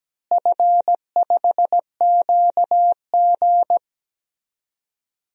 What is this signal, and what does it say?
Morse code, heard as a keyed tone, about 17 wpm, 700 Hz. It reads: F5QG
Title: DXCC-100 Call Signs - 4 Characters - Repeated 2x - 17wpm